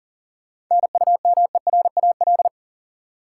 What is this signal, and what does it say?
Morse code, heard as a keyed tone, about 40 wpm, 700 Hz. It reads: NUMERAL
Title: Sentences from Top 1000 Words - Review Part 2 - New Word Drill - 40wpm